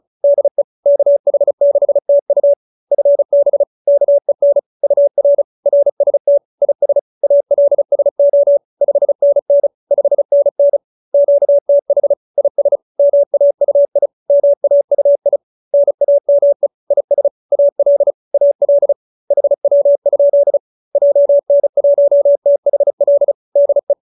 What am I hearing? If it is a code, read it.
Morse code, heard as a keyed tone, about 34 wpm, 570 Hz. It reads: DE KH6TU FB KEN UR RST IS ALSO 5NN 5NN QTH IS MAUI MAUI NAME IS AL AL HW? JN1THL DE